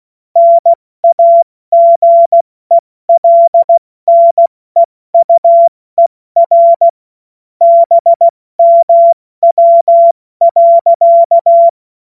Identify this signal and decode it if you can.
Morse code, heard as a keyed tone, about 16 wpm, 680 Hz. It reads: NAGELNEUER BMW.
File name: Morsekurs / Witze / Witz 02. August 2022 70 F80